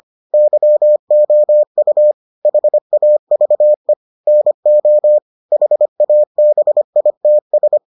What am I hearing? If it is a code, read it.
Morse code, heard as a keyed tone, about 25 wpm, 600 Hz. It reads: YOU HAVE NO HABITS